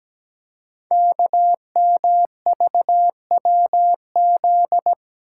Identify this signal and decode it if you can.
Morse code, heard as a keyed tone, about 17 wpm, 700 Hz. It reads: KMVWZ
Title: Code Groups - 5 Letters - 17wpm